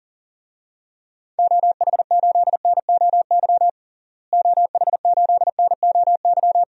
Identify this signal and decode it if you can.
Morse code, heard as a keyed tone, about 40 wpm, 700 Hz. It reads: OH8NOY OH8NOY
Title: DXCC-100 Call Signs - 6 Characters - Repeated 3x - 40wpm